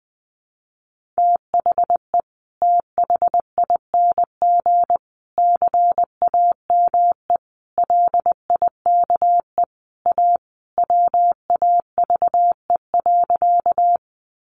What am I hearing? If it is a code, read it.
Morse code, heard as a keyed tone, about 20 wpm, 700 Hz. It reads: THE THING CAME LIKE A WAVE.
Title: Sentences from Top 1400 Words 20wpm